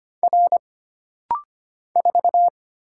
Morse code, heard as a keyed tone, about 25 wpm, 700 Hz. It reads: R 4
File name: Single Character (Letter:Number:Punctuation:Prosign) - Rapid-Fire - 25wpm